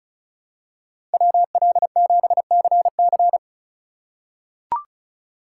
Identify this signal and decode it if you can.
Morse code, heard as a keyed tone, about 34 wpm, 700 Hz. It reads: WL7CC
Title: USA Call Signs - 5 Characters - 4x Character Spacing - 35wpm